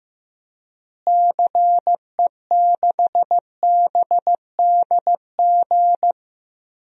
Morse code, heard as a keyed tone, about 16 wpm, 700 Hz. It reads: CE6BDG
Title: DXCC-100 Call Signs - 6 Characters - Repeated 3x - 15wpm